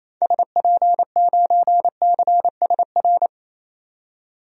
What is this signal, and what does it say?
Morse code, heard as a keyed tone, about 28 wpm, 700 Hz. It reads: SP9CSR
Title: DXCC-100 Call Signs - 6 Characters - 28wpm